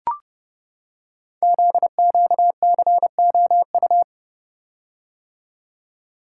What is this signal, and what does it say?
Morse code, heard as a keyed tone, about 30 wpm, 700 Hz. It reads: ZQCOU
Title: Code Groups - 5 Letters - 30wpm